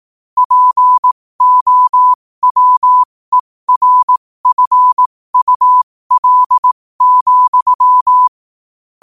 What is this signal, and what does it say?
Morse code, heard as a keyed tone, about 18 wpm, spaced as for 15 wpm, 1000 Hz. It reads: POWERFUL,
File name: qotdfiles / 15wpm